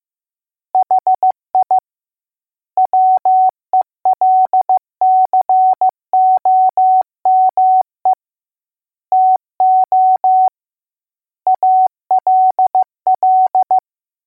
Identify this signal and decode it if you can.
Morse code, heard as a keyed tone, about 15 wpm, 750 Hz. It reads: HI WELCOME TO ALL